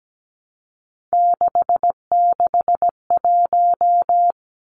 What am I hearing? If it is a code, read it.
Morse code, heard as a keyed tone, about 17 wpm, 700 Hz. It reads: 661
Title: Numbers - 3-Digit 17wpm